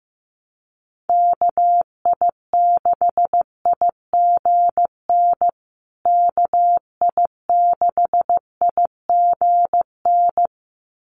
Morse code, heard as a keyed tone, about 15 wpm, 700 Hz. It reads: KI6IGN KI6IGN
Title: USA Call Signs Repeated 2x 15wpm